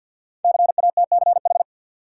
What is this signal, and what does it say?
Morse code, heard as a keyed tone, about 53 wpm, 700 Hz. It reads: CATCH
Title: Sentences from Top 1500 Words - New Word Drill - 50wpm